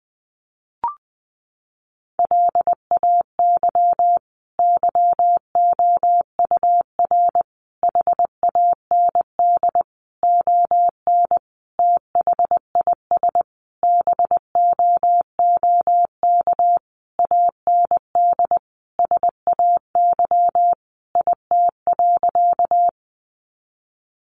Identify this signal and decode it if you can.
Morse code, heard as a keyed tone, about 20 wpm, 700 Hz. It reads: LAY YOUR HAND ON THIS BOOK AND SAY IT.
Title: Sentences from Top 400 Words 20wpm